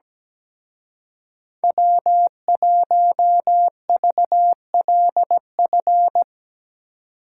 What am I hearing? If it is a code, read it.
Morse code, heard as a keyed tone, about 18 wpm, 700 Hz. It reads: W1VLF
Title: Speed Racing - USA Call Signs 17wpm